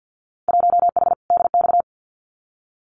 Morse code, heard as a keyed tone, about 50 wpm, 700 Hz. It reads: 1H DX
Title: Winter Field Day Exchanges 50wpm